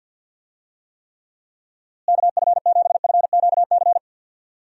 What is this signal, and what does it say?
Morse code, heard as a keyed tone, about 51 wpm, 700 Hz. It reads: KF7FQC
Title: USA Call Signs - 6 Characters - Repeated 2x - 50wpm